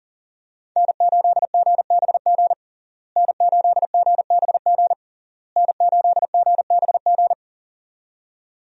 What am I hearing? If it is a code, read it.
Morse code, heard as a keyed tone, about 40 wpm, 700 Hz. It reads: N8GBG N8GBG N8GBG
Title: USA Call Signs - 5 Characters - Repeated 3x - 40wpm